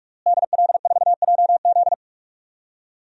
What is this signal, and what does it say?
Morse code, heard as a keyed tone, about 46 wpm, 700 Hz. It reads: DL4JZ
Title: DXCC-100 Call Signs - 5 Characters - Repeated 3x - 45wpm